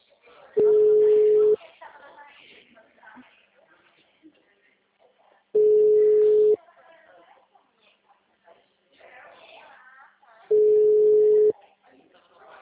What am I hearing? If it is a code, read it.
Ringback tone (Europe, 425 Hz), with speech.
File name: out- -2002- - - .685